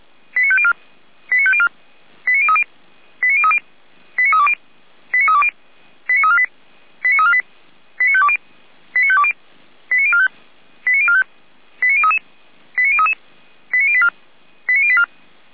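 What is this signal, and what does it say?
Five-tone selective call (ZVEI), heard as quick runs of five tones, tones of about 70 ms. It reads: 89504 89504 89030 89030 89320 89320 89348 89348 89520 89520 89045 89045 89033 89033 89084 89084
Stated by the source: ZVEI five-tone call received by SDR.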